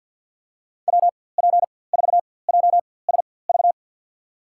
Morse code, heard as a keyed tone, about 51 wpm, spaced as for 23 wpm, 700 Hz. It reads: WP4JSV
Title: USA Call Signs - 6 Characters - 4x Character Spacing - 50wpm